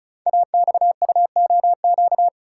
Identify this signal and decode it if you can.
Morse code, heard as a keyed tone, about 36 wpm, 690 Hz. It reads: AXUOQ
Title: Code Groups - 5 Letters - 35wpm